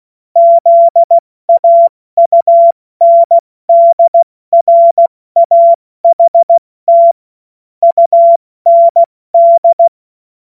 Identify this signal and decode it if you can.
Morse code, heard as a keyed tone, about 16 wpm, spaced as for 14 wpm, 680 Hz. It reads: ZAUNDRAHT UND
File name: Morsekurs / Witze / Witz 17 Juni Mai 2025 70 F80